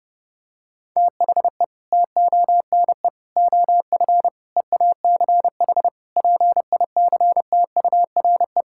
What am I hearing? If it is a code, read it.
Morse code, heard as a keyed tone, about 30 wpm, 700 Hz. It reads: THE TONE OF EACH PICTURE
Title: Sentences from Top 1600 Words 30wpm